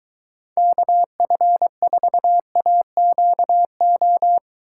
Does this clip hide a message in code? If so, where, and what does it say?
Yes: Morse code from 0.6 to 4.4 s, heard as a keyed tone, about 22 wpm, 700 Hz. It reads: KF4AQO